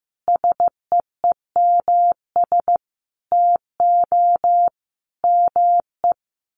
Morse code, heard as a keyed tone, about 15 wpm, 700 Hz. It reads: SEEMS TO ME
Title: Sets of 5 Words Encore2 15wpm